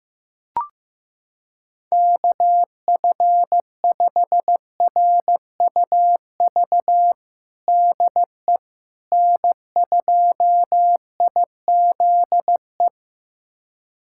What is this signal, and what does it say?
Morse code, heard as a keyed tone, about 15 wpm, 700 Hz. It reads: KF5RUV DE N2IZE
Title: USA Callsign DE Callsign 15wpm